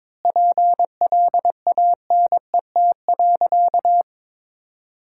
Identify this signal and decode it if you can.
Morse code, heard as a keyed tone, about 21 wpm, 700 Hz. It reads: PLANET.
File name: Difficult Sentences with Hints - 22wpm